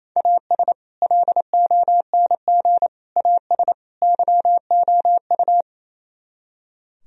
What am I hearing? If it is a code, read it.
Morse code, heard as a keyed tone, about 34 wpm, 700 Hz. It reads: AS LONG AS YOU
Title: Speed-Racing - Sets of 4 Words 28wpm